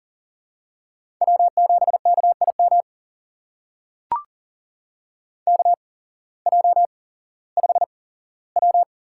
Morse code, heard as a keyed tone, about 39 wpm, spaced as for 19 wpm, 700 Hz. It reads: W7KIM KJ5W
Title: USA Call Signs - 5 Characters - 8x Character Spacing - 40wpm